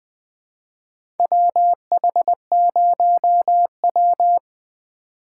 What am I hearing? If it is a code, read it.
Morse code, heard as a keyed tone, about 20 wpm, 700 Hz. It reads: WH0W
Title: USA Call Signs - 4 Characters - 4x Character Spacing - 20wpm